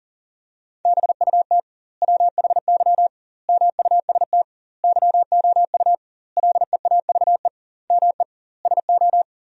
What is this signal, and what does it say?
Morse code, heard as a keyed tone, about 40 wpm, 700 Hz. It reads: BUT WHY MUST YOU LEAVE ME SO
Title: 2x Word-Spacing - Sentences from Top 400 Words - 40wpm